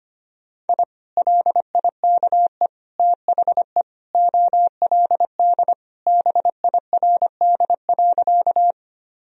Morse code, heard as a keyed tone, about 25 wpm, 700 Hz. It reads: I LIKE THE OLD BIRD.
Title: Sentences from Top 1000 Words - Review Part 2 - 25wpm